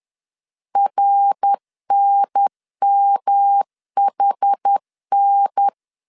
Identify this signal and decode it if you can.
Morse code, heard as a keyed tone, about 10 wpm, 800 Hz. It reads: RNMHN
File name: CW / DET / 1 3 1500C S B 10wpm 3335